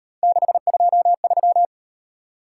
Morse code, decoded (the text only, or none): 623